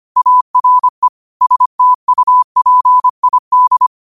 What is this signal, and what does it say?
Morse code, heard as a keyed tone, about 25 wpm, 1000 Hz. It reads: ARE STUPID
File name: qotdfiles / 25wpm